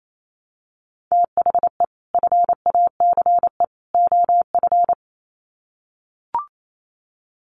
Morse code, heard as a keyed tone, about 28 wpm, 700 Hz. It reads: THE FACE OF
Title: Sets of 3 Words 28wpm